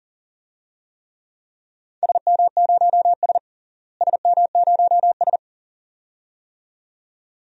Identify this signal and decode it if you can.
Morse code, heard as a keyed tone, about 40 wpm, 700 Hz. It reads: SM0S SM0S
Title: DXCC-100 Call Signs - 4 Characters - Repeated 2x - 40wpm